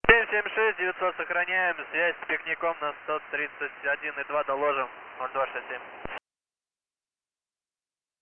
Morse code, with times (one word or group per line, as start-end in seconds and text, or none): none